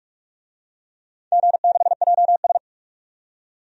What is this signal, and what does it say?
Morse code, heard as a keyed tone, about 46 wpm, 700 Hz. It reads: G6JS
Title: DXCC-100 Call Signs - 4 Characters - 45wpm